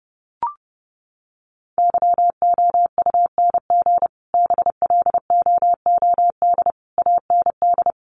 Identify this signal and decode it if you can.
Morse code, heard as a keyed tone, about 30 wpm, 700 Hz. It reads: YOUNG BLOOD AND
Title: Sentences from Top 1100 Words 30wpm